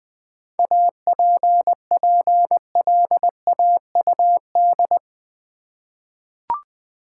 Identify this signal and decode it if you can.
Morse code, heard as a keyed tone, about 20 wpm, 700 Hz. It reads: APPLAUD
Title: 7 Letter Words - First 3 Letters Given - 20wpm